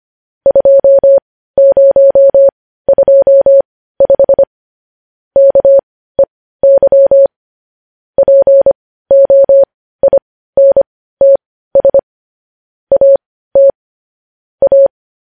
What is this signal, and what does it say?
Morse code, heard as a keyed tone, about 25 wpm, spaced as for 15 wpm, 550 Hz. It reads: 2025 KEY POINTS AT A
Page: News Headlines in Morse Code at 15 WPM